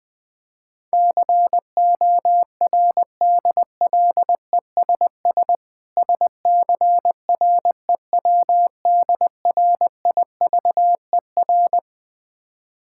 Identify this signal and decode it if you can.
Morse code, heard as a keyed tone, about 20 wpm, 700 Hz. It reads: CORDLESS SCREWDRIVER
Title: Difficult Words with Hints - 20wpm